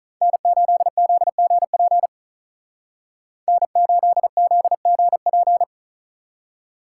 Morse code, heard as a keyed tone, about 46 wpm, 700 Hz. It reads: N8ZGP N8ZGP